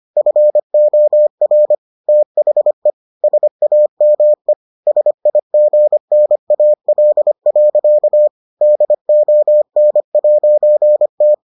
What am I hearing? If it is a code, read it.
Morse code, heard as a keyed tone, about 25 wpm, 600 Hz. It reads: FOR THE SAME SIGNAL. DON'T